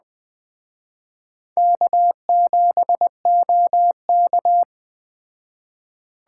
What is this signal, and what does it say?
Morse code, heard as a keyed tone, about 20 wpm, 700 Hz. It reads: K7OK